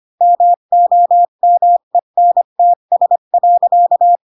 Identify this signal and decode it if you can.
Morse code, heard as a keyed tone, about 25 wpm, 700 Hz. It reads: MOMENTS.
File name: Pinocchio / Pinocchio ch5 20 25 ik1hge